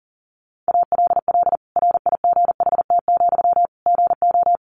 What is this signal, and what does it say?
Morse code, heard as a keyed tone, about 40 wpm, 700 Hz. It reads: ALL RIGHT, GO
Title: Sentences from Top 1100 Words 40wpm